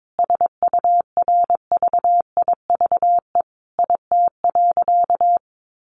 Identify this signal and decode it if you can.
Morse code, heard as a keyed tone, about 22 wpm, 700 Hz. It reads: SURVIVE IT.
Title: Sentences from Top 1300 Words 22wpm